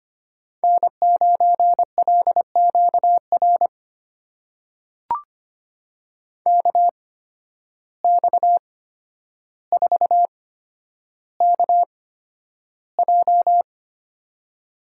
Morse code, heard as a keyed tone, about 25 wpm, spaced as for 11 wpm, 700 Hz. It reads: N9LQR KX4KJ
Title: USA Call Signs - 5 Characters - 8x Character Spacing - 25wpm